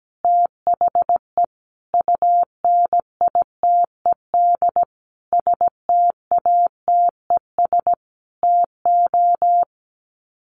Morse code, heard as a keyed tone, about 17 wpm, 700 Hz. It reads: THE UNITED STATES TO
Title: Sets of 5 Words Encore 17wpm